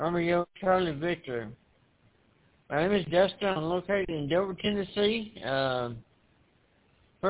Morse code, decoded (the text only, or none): none